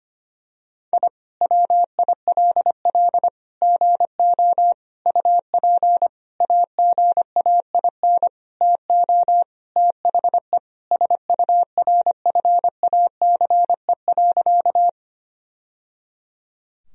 Morse code, heard as a keyed tone, about 25 wpm, 700 Hz. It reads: I WILL GO UP AGAIN TO THE SURFACE.
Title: Sentences from Top 1000 Words - Review Part 3 - 25wpm